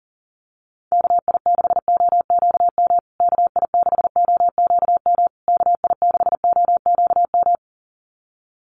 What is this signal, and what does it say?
Morse code, heard as a keyed tone, about 40 wpm, 700 Hz. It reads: KI6OQM KI6OQM KI6OQM
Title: USA Call Signs Repeated 3x 40wpm